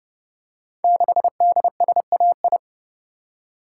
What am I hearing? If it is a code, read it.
Morse code, heard as a keyed tone, about 30 wpm, 700 Hz. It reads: 6DSAI
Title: Code Groups - 5 Letter:Numbers - 30wpm